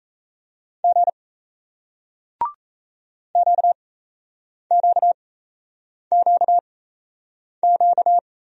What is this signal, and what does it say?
Morse code, heard as a keyed tone, about 38 wpm, spaced as for 8 wpm, 700 Hz. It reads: G QQQQ